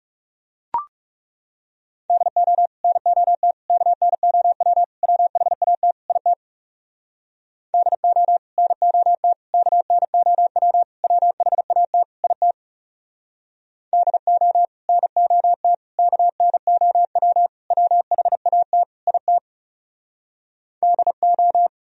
Morse code, heard as a keyed tone, about 37 wpm, 700 Hz. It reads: DO NOT KNOW WHAT IT DO NOT KNOW WHAT IT DO NOT KNOW WHAT IT DO
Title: Speed-Racing - Sets of 5 Words 30wpm